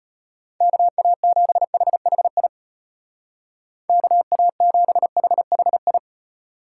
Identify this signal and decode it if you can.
Morse code, heard as a keyed tone, about 30 wpm, 700 Hz. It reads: KA7HHI KA7HHI
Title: Speed Racing - USA Call Signs 25wpm